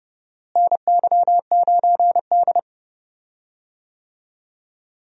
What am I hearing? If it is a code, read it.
Morse code, heard as a keyed tone, about 32 wpm, 700 Hz. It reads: NY9D
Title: USA Call Signs - 4 Characters - Repeated 2x - 30wpm